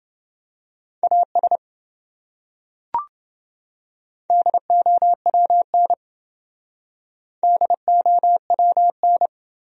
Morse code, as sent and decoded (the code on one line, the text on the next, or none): .- ... / -.. --- .-- -. / -.. --- .-- -.
AS DOWN DOWN